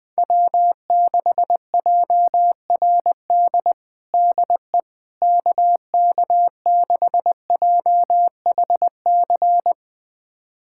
Morse code, heard as a keyed tone, about 21 wpm, 700 Hz. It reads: W6JRD DE KK6JHC
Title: USA Callsign DE Callsign 20wpm